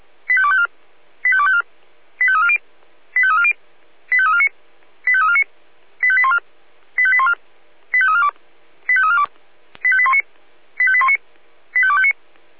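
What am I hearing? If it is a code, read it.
Five-tone selective call (ZVEI), heard as quick runs of five tones, tones of about 70 ms. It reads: 85254 85254 85340 85340 85349 85349 85814 85814 85342 85342 85819 85819 85259
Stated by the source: ZVEI five-tone selective call received by SDR.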